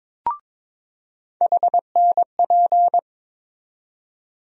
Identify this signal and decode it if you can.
Morse code, heard as a keyed tone, about 22 wpm, 700 Hz. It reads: HNP
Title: USA Call Sign Postfixes - 3 Letters 22wpm